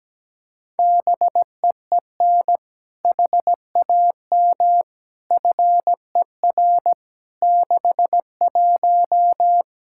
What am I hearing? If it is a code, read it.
Morse code, heard as a keyed tone, about 17 wpm, 700 Hz. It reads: BEEN HAM FER 61
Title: Practice QSO - Exchange 3 - More Info - 17wpm